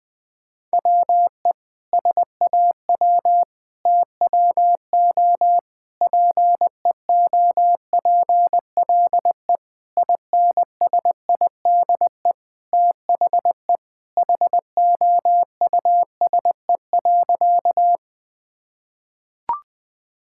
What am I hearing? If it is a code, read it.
Morse code, heard as a keyed tone, about 20 wpm, 700 Hz. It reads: WE SAW TWO PEOPLE INSIDE THE HOUSE.
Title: Sentences from Top 1000 Words - Review Part 6 - 20wpm